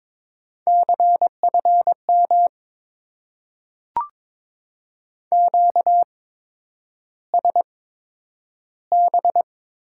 Morse code, heard as a keyed tone, about 22 wpm, spaced as for 10 wpm, 690 Hz. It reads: CFM QSB
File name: Top 90 QSO Elements - 8x Character Spacing - 22wpm